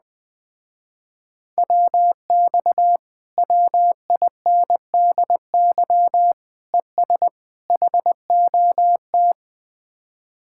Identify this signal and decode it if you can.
Morse code, heard as a keyed tone, about 20 wpm, 700 Hz. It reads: WX WINDY ES HOT